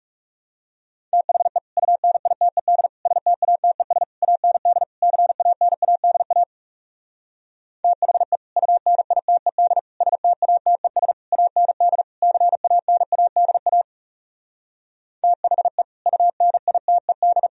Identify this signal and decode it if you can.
Morse code, heard as a keyed tone, about 39 wpm, 700 Hz. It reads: THE UNITED STATES AND CANADA THE UNITED STATES AND CANADA THE UNITED